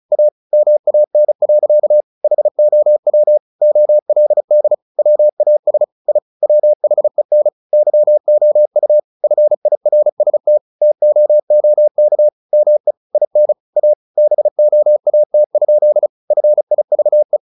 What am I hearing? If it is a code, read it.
Morse code, heard as a keyed tone, about 34 wpm, 600 Hz. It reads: A MAN. HOW OLD WAS I WHEN YOU FIRST TOOK ME IN A BOAT? FIVE
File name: the old man and the sea / the old man and the sea 1 35wpm